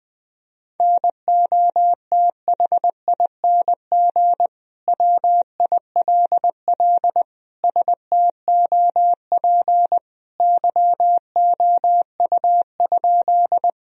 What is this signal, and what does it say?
Morse code, heard as a keyed tone, about 21 wpm, 700 Hz. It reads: NOTHING WILL STOP YOU?
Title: Sentences from Top 1000 Words - Review Part 1 - 20wpm